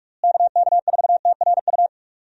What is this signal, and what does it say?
Morse code, heard as a keyed tone, about 41 wpm, 700 Hz. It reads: KK4TRU
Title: USA Call Signs - 6 Characters - Repeated 3x - 45wpm